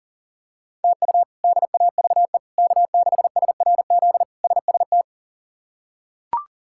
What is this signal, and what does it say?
Morse code, heard as a keyed tone, about 40 wpm, 700 Hz. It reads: TU DAVE K6SRZ SST
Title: K1USN SST QSO Practice - 40wpm